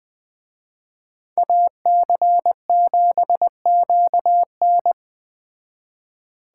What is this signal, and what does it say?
Morse code, heard as a keyed tone, about 21 wpm, 700 Hz. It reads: AC7QN
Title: Speed Racing - USA 2x2 Call Signs 17wpm